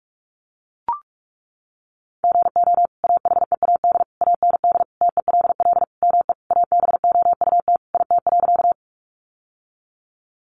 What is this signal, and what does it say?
Morse code, heard as a keyed tone, about 44 wpm, 700 Hz. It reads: GO AHEAD AND TELL ME ABOUT IT.
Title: Sentences from Top 1100 Words 45wpm